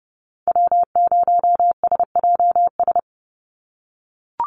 Morse code, heard as a keyed tone, about 32 wpm, 700 Hz. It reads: W0SJS
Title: Call Signs - 30wpm